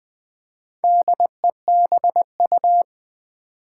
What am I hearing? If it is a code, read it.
Morse code, heard as a keyed tone, about 20 wpm, 700 Hz. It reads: DEBU
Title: Build A Word - 1 Letter At A Time - 5-Letter Words - 20wpm